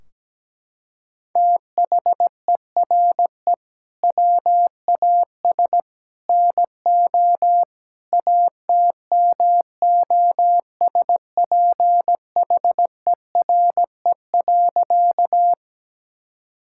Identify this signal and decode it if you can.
Morse code, heard as a keyed tone, about 17 wpm, 700 Hz. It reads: THERE WAS NO ATMOSPHERE.
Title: Sentences from Top 1600 Words 17wpm